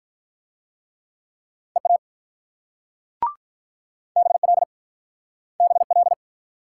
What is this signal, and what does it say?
Morse code, heard as a keyed tone, about 46 wpm, 700 Hz. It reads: EA BL BL